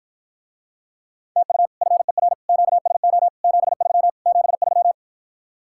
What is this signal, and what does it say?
Morse code, heard as a keyed tone, about 54 wpm, 700 Hz. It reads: TU FER QSO 73 73